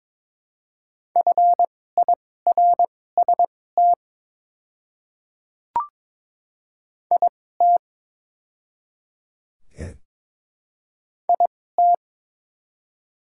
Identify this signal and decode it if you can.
Morse code, heard as a keyed tone, about 22 wpm, spaced as for 16 wpm, 700 Hz. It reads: FIRST IT IT
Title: 2x Character Spacing - Top 100 Words - 22wpm